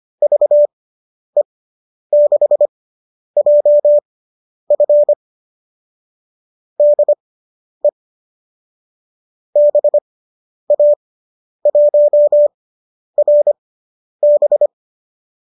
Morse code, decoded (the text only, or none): VE6JF DE BA1RB